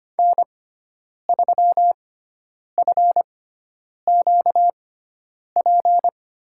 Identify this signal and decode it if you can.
Morse code, heard as a keyed tone, about 25 wpm, spaced as for 9 wpm, 700 Hz. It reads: N3FQP
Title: USA Call Signs - 5 Characters - 6x Character Spacing - 25wpm